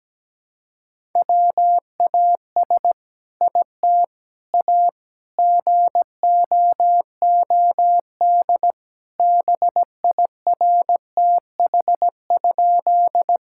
Morse code, heard as a keyed tone, about 17 wpm, 700 Hz. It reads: WAS IT A GOOD BIRTH?
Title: Sentences from Top 1000 Words - Review Part 5 - 17wpm